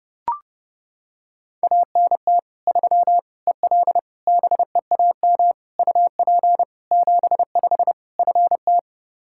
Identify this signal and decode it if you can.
Morse code, heard as a keyed tone, about 30 wpm, 700 Hz. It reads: ANT 3 EL BEAM UP 75 FT
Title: Practice QSO - Exchange 2 - Info - 30wpm